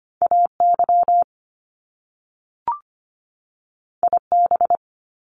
Morse code, heard as a keyed tone, about 25 wpm, 700 Hz. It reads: AY IB
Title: Top Two Letter Combinations in Words 25wpm